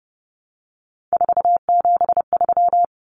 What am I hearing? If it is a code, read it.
Morse code, heard as a keyed tone, about 30 wpm, 700 Hz. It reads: 473
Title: Numbers - 3-Digit 30wpm